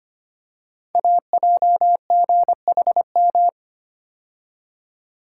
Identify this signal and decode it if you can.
Morse code, heard as a keyed tone, about 25 wpm, 700 Hz. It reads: AJGHM
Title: Code Groups - 5 Letters - 25wpm